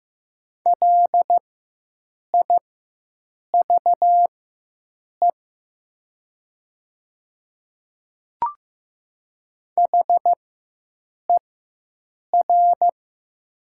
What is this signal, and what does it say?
Morse code, heard as a keyed tone, about 15 wpm, spaced as for 7 wpm, 700 Hz. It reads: LIVE HER
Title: 4x Character Spacing - Top 200 Words - 15wpm